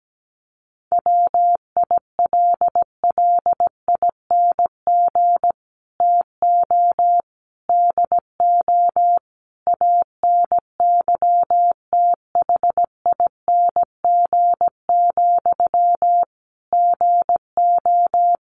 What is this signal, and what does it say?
Morse code, heard as a keyed tone, about 17 wpm, 700 Hz. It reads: WILLING TO DO ANYTHING, GO
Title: Sentences from Top 1400 Words 17wpm